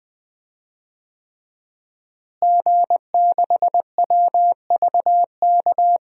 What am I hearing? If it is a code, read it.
Morse code, heard as a keyed tone, about 21 wpm, 700 Hz. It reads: G6WVK
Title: DXCC-100 Call Signs - 5 Characters - Repeated 3x - 20wpm